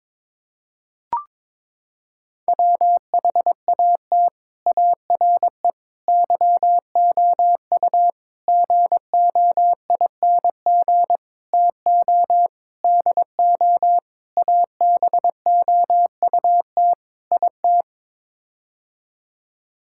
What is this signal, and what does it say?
Morse code, heard as a keyed tone, about 21 wpm, 700 Hz. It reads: WHAT ARE YOU GOING TO DO ABOUT IT
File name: practice / 22wpm / Sets of 8 Words - 22wpm